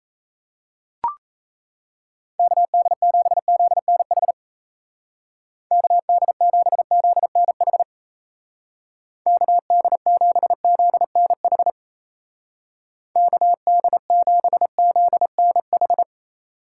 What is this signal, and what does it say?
Morse code, heard as a keyed tone, about 33 wpm, 700 Hz. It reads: KD7ZNH KD7ZNH KD7ZNH KD7ZNH
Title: Speed Racing - USA 2x3 Call Signs 28wpm